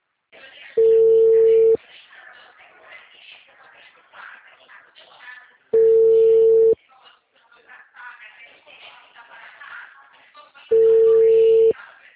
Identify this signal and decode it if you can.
Ringback tone (China, 450 Hz), with speech.